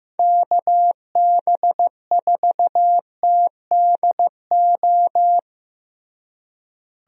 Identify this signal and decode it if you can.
Morse code, heard as a keyed tone, about 15 wpm, 700 Hz. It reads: KB4TDO
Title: USA Call Signs - 6 Characters - Repeated 3x - 15wpm